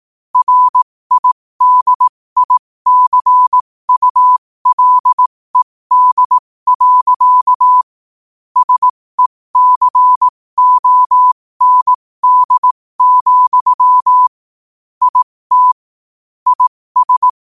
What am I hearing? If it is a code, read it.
Morse code, heard as a keyed tone, about 18 wpm, spaced as for 15 wpm, 1000 Hz. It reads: RIDICULED. SECOND, IT IS
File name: qotdfiles / 15wpm